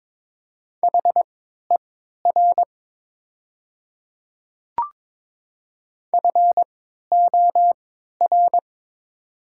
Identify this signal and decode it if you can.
Morse code, heard as a keyed tone, about 23 wpm, spaced as for 13 wpm, 700 Hz. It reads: HER FOR
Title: 3x Character Spacing - Top 100 Words - 22wpm